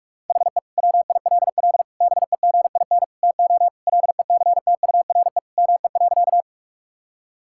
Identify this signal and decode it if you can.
Morse code, heard as a keyed tone, about 46 wpm, 700 Hz. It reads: HE WILL BEGIN TO LECTURE ME.
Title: Sentences from Top 1600 Words 45wpm